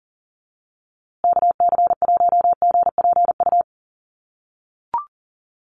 Morse code, heard as a keyed tone, about 40 wpm, 700 Hz. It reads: KC1GPU
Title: USA Call Signs Repeated 2x 40wpm